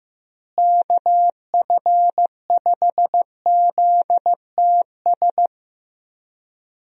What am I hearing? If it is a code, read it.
Morse code, heard as a keyed tone, about 15 wpm, 700 Hz. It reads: KF5ZTS
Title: USA Call Signs - 6 Characters - Repeated 3x - 15wpm